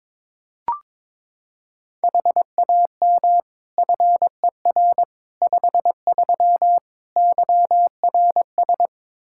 Morse code, heard as a keyed tone, about 22 wpm, 700 Hz. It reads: HAM FER 53 YRS
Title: Practice QSO - Exchange 3 - More Info - 22wpm